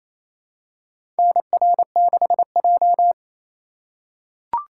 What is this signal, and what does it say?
Morse code, heard as a keyed tone, about 27 wpm, 700 Hz. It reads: NR6J
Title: USA Call Signs - 4 Characters - 4x Character Spacing - 28wpm